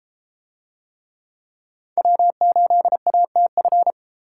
Morse code, heard as a keyed tone, about 30 wpm, 700 Hz. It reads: W8ATF